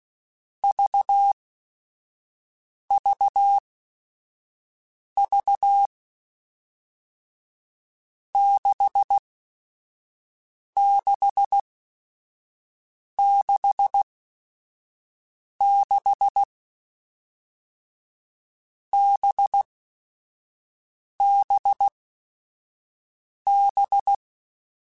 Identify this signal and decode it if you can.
Morse code, heard as a keyed tone, about 16 wpm, spaced as for 5 wpm, 800 Hz. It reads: VVV 6666 BBB
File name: Groups@5 / Groups@5 mp3s / VB46@5